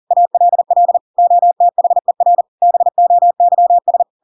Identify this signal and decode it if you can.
Morse code, heard as a keyed tone, about 40 wpm, 700 Hz. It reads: ALL OTHER BOYS
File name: Pinocchio / Pinocchio ch4 40 40 ik1hge